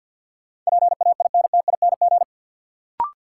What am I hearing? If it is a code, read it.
Morse code, heard as a keyed tone, about 53 wpm, 700 Hz. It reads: PAINTING
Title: Sentences from Top 1000 Words - Review Part 6 - New Word Drill - 50wpm